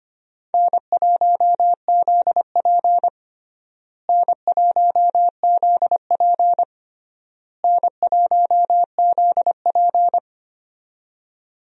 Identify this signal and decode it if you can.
Morse code, heard as a keyed tone, about 25 wpm, 700 Hz. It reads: N1ZP N1ZP N1ZP
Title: USA Call Signs - 4 Characters - Repeated 3x - 25wpm